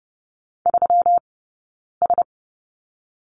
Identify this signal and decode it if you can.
Morse code, heard as a keyed tone, about 30 wpm, 700 Hz. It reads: 3 S
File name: Lesson 20d - Number 3 - Two Characters 30wpm